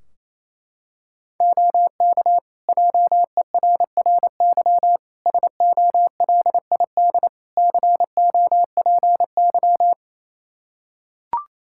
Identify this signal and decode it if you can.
Morse code, heard as a keyed tone, about 28 wpm, 700 Hz. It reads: OK JERRY SOLID COPY
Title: Practice QSO - Exchange 3 - More Info - 28wpm